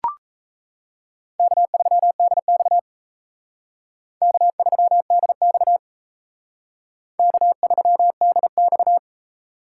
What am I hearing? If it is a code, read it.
Morse code, heard as a keyed tone, about 35 wpm, 700 Hz. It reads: K3DX K3DX K3DX
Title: Speed Racing - USA 1x2 Call Signs 28wpm